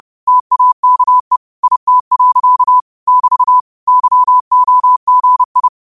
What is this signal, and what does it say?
Morse code, heard as a keyed tone, about 30 wpm, 1000 Hz. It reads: TAKE IT. <BT> YOGI
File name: qotdfiles / 30wpm